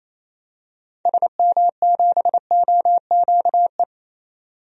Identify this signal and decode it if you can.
Morse code, heard as a keyed tone, about 28 wpm, 700 Hz. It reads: SM7OQE
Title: DXCC-100 Call Signs - 6 Characters - 28wpm